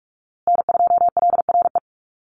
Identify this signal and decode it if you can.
Morse code, heard as a keyed tone, about 44 wpm, 700 Hz. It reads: N2LRE
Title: USA Call Signs Repeated 3x 45wpm